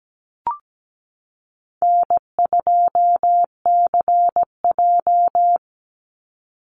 Morse code, heard as a keyed tone, about 17 wpm, 700 Hz. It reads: N2CJ
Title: Call Signs - 17wpm